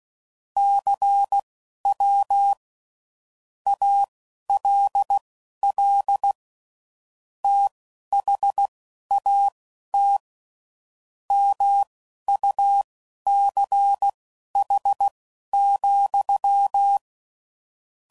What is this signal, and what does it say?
Morse code, heard as a keyed tone, about 16 wpm, spaced as for 11 wpm, 800 Hz. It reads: CW ALL THAT MUCH,
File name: CW Code Groups @10WPM / QSOs / QSO WD0FBT